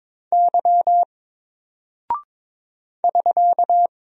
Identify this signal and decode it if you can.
Morse code, heard as a keyed tone, about 21 wpm, 700 Hz. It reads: Y <SK>
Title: Single Character (Letter:Number:Punctuation:Prosign) Alt 22wpm